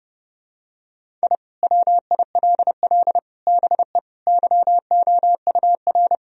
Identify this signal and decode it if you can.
Morse code, heard as a keyed tone, about 30 wpm, 700 Hz. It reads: I WILL BE YOUR
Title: Sentences from Top 1000 Words - Review Part 4 - 30wpm